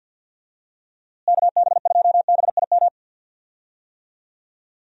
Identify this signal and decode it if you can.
Morse code, heard as a keyed tone, about 48 wpm, 700 Hz. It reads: KB2BIM
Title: USA Call Signs - 6 Characters - Repeated 2x - 50wpm